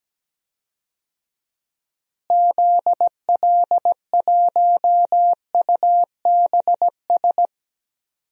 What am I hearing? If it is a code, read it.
Morse code, heard as a keyed tone, about 17 wpm, 700 Hz. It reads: ZL1UBS
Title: DXCC-100 Call Signs - 6 Characters - Repeated 3x - 17wpm